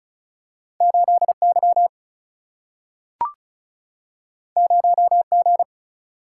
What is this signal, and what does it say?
Morse code, heard as a keyed tone, about 36 wpm, 700 Hz. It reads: 8Y 0G
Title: Single-Letter Single-Number 35wpm